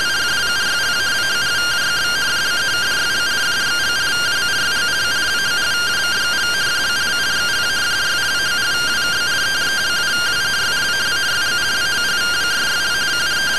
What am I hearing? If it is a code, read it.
Two-tone teleprinter signal, about 170 Hz shift.